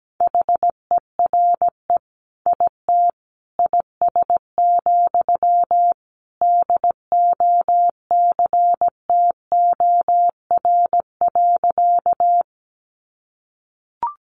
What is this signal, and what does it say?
Morse code, heard as a keyed tone, about 17 wpm, 700 Hz. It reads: HERE IT IS, DOCTOR.
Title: Sentences from Top 1500 Words 17wpm